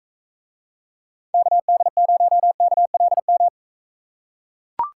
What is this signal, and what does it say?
Morse code, heard as a keyed tone, about 42 wpm, 700 Hz. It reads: KD0KLM